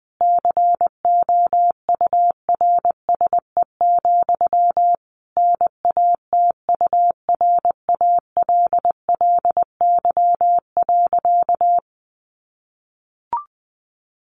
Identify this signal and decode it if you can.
Morse code, heard as a keyed tone, about 20 wpm, 700 Hz. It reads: COURSE, NATURALLY.
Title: Sentences from Top 1300 Words 20wpm